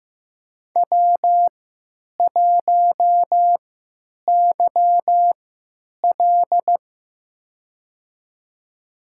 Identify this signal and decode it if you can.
Morse code, heard as a keyed tone, about 15 wpm, spaced as for 9 wpm, 700 Hz. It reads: W1YL
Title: USA Call Signs - 3x Character Spacing - 15wpm